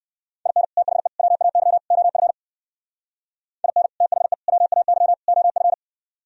Morse code, heard as a keyed tone, about 58 wpm, 700 Hz. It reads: IN THE WAY OF IN THE WAY OF